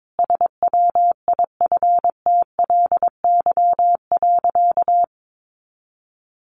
Morse code, heard as a keyed tone, about 22 wpm, 700 Hz. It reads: SWIFTLY.
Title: A Sound Of Thunder 22wpm